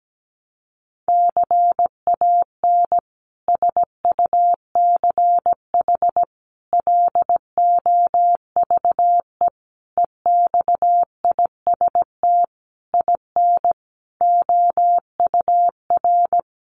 Morse code, heard as a keyed tone, about 17 wpm, 700 Hz. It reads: CAN SUCH LOVE EXIST IN OUR
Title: Sentences from Top 1100 Words 17wpm